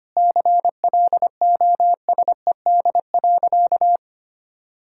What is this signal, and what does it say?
Morse code, heard as a keyed tone, about 25 wpm, 700 Hz. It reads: CLOSED.
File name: practice / 25wpm / Sentences with 3-Word Phrases - 25wpm